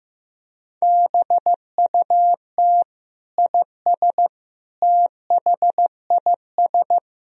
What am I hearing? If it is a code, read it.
Morse code, heard as a keyed tone, about 15 wpm, 700 Hz. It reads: BUT IS THIS
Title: Sentences from Top 1000 Words - Review Part 3 - 15wpm